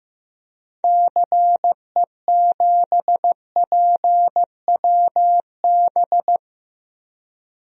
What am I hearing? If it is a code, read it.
Morse code, heard as a keyed tone, about 15 wpm, 700 Hz. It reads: CE7PWB
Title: DXCC-100 Call Signs - 6 Characters - Repeated 2x - 15wpm